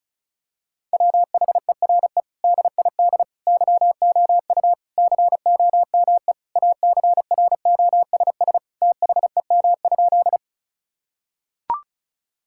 Morse code, heard as a keyed tone, about 36 wpm, 700 Hz. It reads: WHERE DID YOU COME ACROSS THEM?
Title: Sentences from Top 1000 Words - Review Part 5 - 35wpm